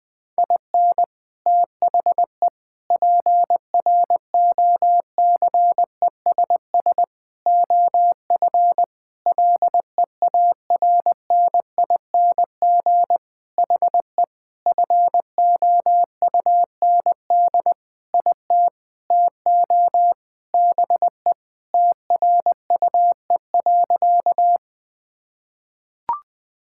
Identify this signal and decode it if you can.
Morse code, heard as a keyed tone, about 20 wpm, 700 Hz. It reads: IN THE PROCESS OF LEARNING HE FOUND IT TO BE TRUE.
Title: Sentences with 4-Word Phrases - Spoken Beginning - 20wpm